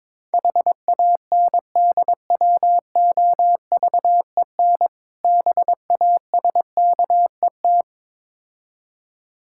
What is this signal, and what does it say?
Morse code, heard as a keyed tone, about 22 wpm, 690 Hz. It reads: HANDWOVEN BASKET
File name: Difficult Words with Hints - 22wpm